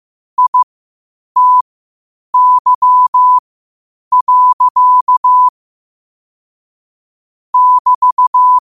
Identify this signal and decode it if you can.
Morse code, heard as a keyed tone, about 15 wpm, spaced as for 8 wpm, 1000 Hz. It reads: ITY. <BT>
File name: qotdfiles / 8wpm